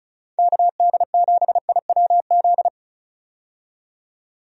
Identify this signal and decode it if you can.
Morse code, heard as a keyed tone, about 36 wpm, 700 Hz. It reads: KD7IWZ
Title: USA Call Signs - 6 Characters - 35wpm